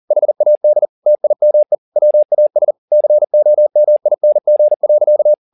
Morse code, heard as a keyed tone, about 40 wpm, 600 Hz. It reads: HAD TIME WAS COMING.
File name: the old man and the sea 2 40wpm